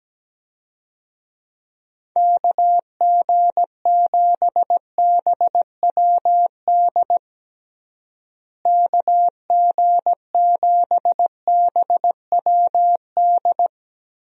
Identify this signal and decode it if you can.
Morse code, heard as a keyed tone, about 17 wpm, 700 Hz. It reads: KG7BWD KG7BWD
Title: USA Call Signs - 6 Characters - Repeated 3x - 17wpm